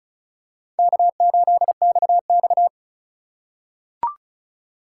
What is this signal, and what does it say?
Morse code, heard as a keyed tone, about 36 wpm, 700 Hz. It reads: K8XX
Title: USA Call Signs - 4 Characters - 2x Character Spacing - 35wpm